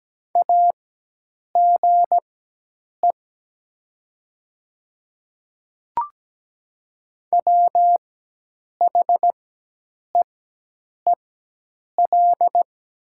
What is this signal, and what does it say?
Morse code, heard as a keyed tone, about 17 wpm, spaced as for 8 wpm, 700 Hz. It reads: AGE WHEEL
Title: 4x Character Spacing - Top 500 Words - 17wpm